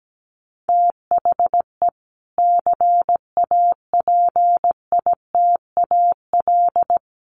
Morse code, heard as a keyed tone, about 17 wpm, 700 Hz. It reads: THE CAPITAL
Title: Sentences from Top 1000 Words 17wpm